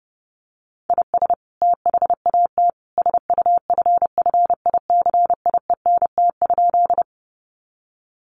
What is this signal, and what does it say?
Morse code, heard as a keyed tone, about 30 wpm, 700 Hz. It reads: IS THAT SUFFICIENT?
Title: Sentences from Top 1400 Words 30wpm